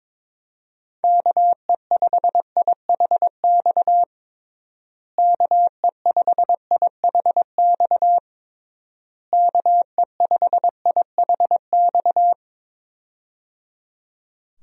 Morse code, heard as a keyed tone, about 22 wpm, 700 Hz. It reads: KE5IHX KE5IHX KE5IHX
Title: USA Call Signs - 6 Characters - Repeated 3x - 22wpm